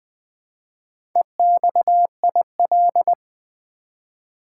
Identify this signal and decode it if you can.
Morse code, heard as a keyed tone, about 20 wpm, 700 Hz. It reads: EXIL
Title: Build A Word - 1 Letter At A Time - 5-Letter Words - 20wpm